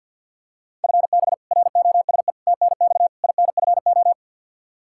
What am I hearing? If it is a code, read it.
Morse code, heard as a keyed tone, about 49 wpm, 700 Hz. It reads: FB ROSE TNX INFO
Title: Practice QSO - Exchange 3 - More Info - 50wpm